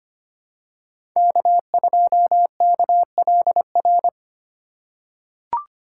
Morse code, heard as a keyed tone, about 25 wpm, 700 Hz. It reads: K2KLR
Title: USA Call Signs - 5 Characters - 3x Character Spacing - 25wpm